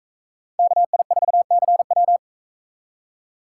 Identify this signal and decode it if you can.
Morse code, heard as a keyed tone, about 41 wpm, 700 Hz. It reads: KI4CW